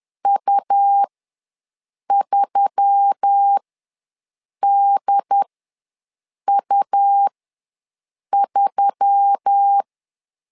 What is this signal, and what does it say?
Morse code, heard as a keyed tone, about 10 wpm, spaced as for 6 wpm, 800 Hz. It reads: U3DU3
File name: LEZIONI / 4 Q1S3 ws 10wpm 2437